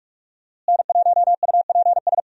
Morse code, heard as a keyed tone, about 44 wpm, 700 Hz. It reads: N1UPS
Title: USA Call Signs - 5 Characters - Repeated 3x - 45wpm